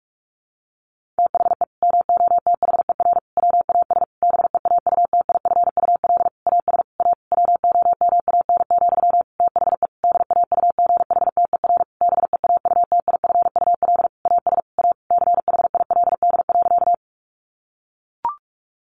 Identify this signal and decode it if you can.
Morse code, heard as a keyed tone, about 44 wpm, 700 Hz. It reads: THE MOTHER WAS BEAUTIFUL AS A WOMAN, THE DAUGHTER BEAUTIFUL AS A CHILD.
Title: Sentences from Top 1500 Words 45wpm